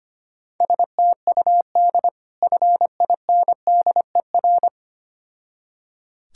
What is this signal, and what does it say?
Morse code, heard as a keyed tone, about 25 wpm, 700 Hz. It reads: STUD FINDER
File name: practice / 25wpm / Difficult Words with Hints - 25wpm